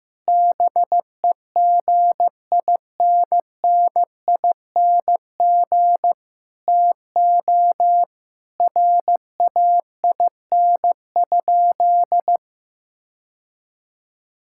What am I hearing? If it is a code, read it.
Morse code, heard as a keyed tone, about 15 wpm, 700 Hz. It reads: BEGINNING TO RAIN?
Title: Sentences from Top 1000 Words - Review Part 5 - 15wpm